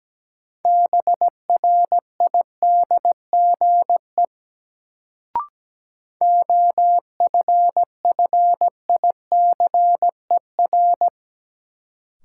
Morse code, heard as a keyed tone, about 17 wpm, 700 Hz. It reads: BRIDGE OFFICER
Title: Sentences from Top 1200 Words - New Word Drill - 17wpm